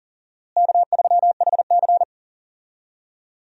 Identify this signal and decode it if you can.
Morse code, heard as a keyed tone, about 40 wpm, 700 Hz. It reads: K3HC
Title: USA Call Signs - 4 Characters - Repeated 3x - 40wpm